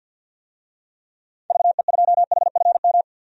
Morse code, heard as a keyed tone, about 53 wpm, 700 Hz. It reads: VE2HFM
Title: DXCC-100 Call Signs - 6 Characters - 50wpm